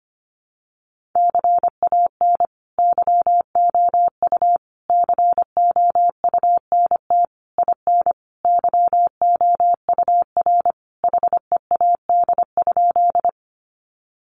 Morse code, heard as a keyed tone, about 25 wpm, 700 Hz. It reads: CAN YOU COUNT IN YOUR HEAD?
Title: Sentences from Top 1500 Words 25wpm